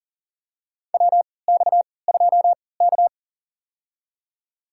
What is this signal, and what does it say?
Morse code, heard as a keyed tone, about 40 wpm, spaced as for 23 wpm, 700 Hz. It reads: WX2K
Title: USA Call Signs - 3x Character Spacing - 40wpm